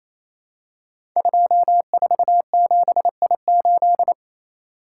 Morse code, heard as a keyed tone, about 27 wpm, 700 Hz. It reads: 247I8
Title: Code Groups - 5 Letter:Numbers - 28wpm